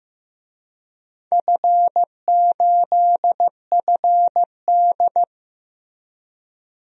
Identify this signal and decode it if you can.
Morse code, heard as a keyed tone, about 15 wpm, 700 Hz. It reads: F8FD
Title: DXCC-100 Call Signs - 4 Characters - Repeated 3x - 15wpm